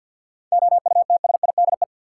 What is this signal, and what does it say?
Morse code, heard as a keyed tone, about 45 wpm, 700 Hz. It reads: OUTSIDE
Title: Seven Letter Words - 50wpm